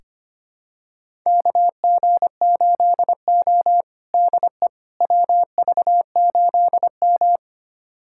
Morse code, heard as a keyed tone, about 25 wpm, 700 Hz. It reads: KG8O DE WV8M
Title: USA Callsign DE Callsign - 4 Characters - 25wpm